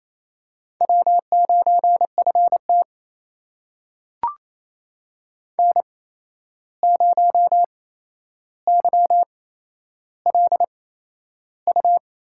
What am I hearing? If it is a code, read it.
Morse code, heard as a keyed tone, about 28 wpm, spaced as for 11 wpm, 700 Hz. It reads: W9FT N0YLU
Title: USA Call Signs - 8x Character Spacing - 28wpm